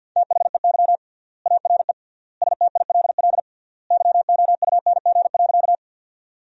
Morse code, heard as a keyed tone, about 51 wpm, 700 Hz. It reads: THEY ARE STILL YOUNG.
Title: 3x Word-Spacing - Sentences from Top 400 Words - 50wpm